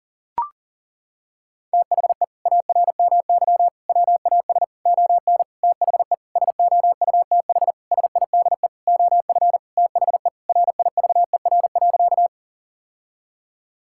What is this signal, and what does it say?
Morse code, heard as a keyed tone, about 40 wpm, 700 Hz. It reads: THE ARMY WAS ON THE SOUTH SIDE OF THE RIVER.
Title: Sentences from Top 1000 Words - Review Part 5 - 40wpm